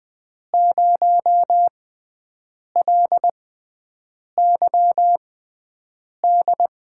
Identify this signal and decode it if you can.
Morse code, heard as a keyed tone, about 20 wpm, spaced as for 7 wpm, 700 Hz. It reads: 0LYD
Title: USA Call Signs - 6 Characters - 6x Character Spacing - 20wpm